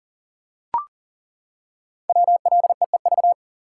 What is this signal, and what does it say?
Morse code, heard as a keyed tone, about 38 wpm, 700 Hz. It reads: WLEEV
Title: Code Groups - 5 Letters - 40wpm